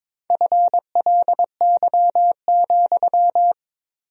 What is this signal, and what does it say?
Morse code, heard as a keyed tone, about 22 wpm, 700 Hz. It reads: FLY,
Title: 4x Word-Spacing - Sentences from Top 500 Words - 22wpm